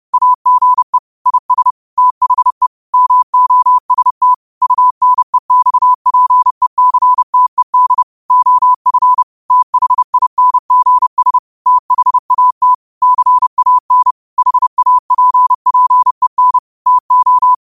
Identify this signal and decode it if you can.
Morse code, heard as a keyed tone, about 30 wpm, 1000 Hz. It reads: AGE IS THE MOST UNEXPECTED OF THINGS THAT CAN HAPPEN TO